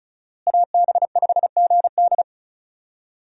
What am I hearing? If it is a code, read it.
Morse code, heard as a keyed tone, about 36 wpm, 700 Hz. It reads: AB5GD
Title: USA Call Signs - 5 Characters - Repeated 2x - 35wpm